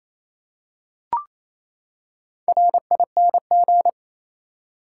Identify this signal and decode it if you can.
Morse code, heard as a keyed tone, about 28 wpm, 700 Hz. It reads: RING
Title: Four Letter Words - 28wpm